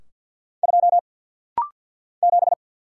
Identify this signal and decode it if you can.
Morse code, heard as a keyed tone, about 48 wpm, 700 Hz. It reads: 2 7
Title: Single Letter-Number - Rapid-Fire 50wpm